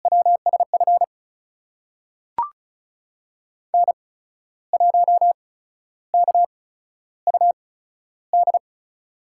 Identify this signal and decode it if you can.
Morse code, heard as a keyed tone, about 34 wpm, spaced as for 13 wpm, 700 Hz. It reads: WSF N1KUD
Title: USA Call Signs - 5 Characters - 8x Character Spacing - 35wpm